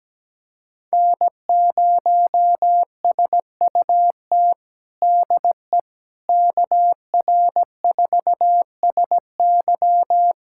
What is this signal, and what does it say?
Morse code, heard as a keyed tone, about 17 wpm, 700 Hz. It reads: N0SUT DE KR4SY
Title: USA Callsign DE Callsign - 5 Characters - 17wpm